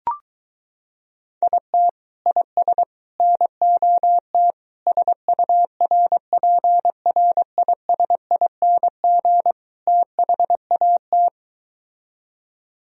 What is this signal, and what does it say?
Morse code, heard as a keyed tone, about 23 wpm, 700 Hz. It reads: IT IS NOT SURPRISING THAT